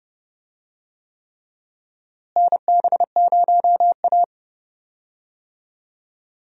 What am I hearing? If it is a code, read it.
Morse code, heard as a keyed tone, about 30 wpm, 700 Hz. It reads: NB0A
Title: USA Call Signs - 4 Characters - Repeated 3x - 30wpm